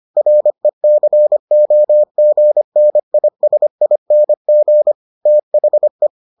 Morse code, heard as a keyed tone, about 25 wpm, 600 Hz. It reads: RECOGNISING THE